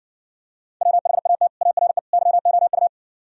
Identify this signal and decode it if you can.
Morse code, heard as a keyed tone, about 66 wpm, 700 Hz. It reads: WHAT ARE YOU